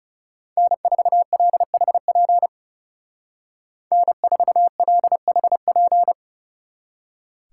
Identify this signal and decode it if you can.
Morse code, heard as a keyed tone, about 35 wpm, 700 Hz. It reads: N4LHP N4LHP
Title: Speed Racing - USA 1x3 Call Signs 30wpm